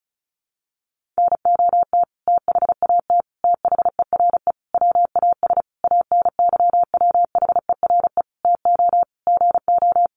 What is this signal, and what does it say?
Morse code, heard as a keyed tone, about 34 wpm, 700 Hz. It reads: NOT THAT THERE WAS ANYWHERE TO GO
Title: Sentences from Top 1400 Words 35wpm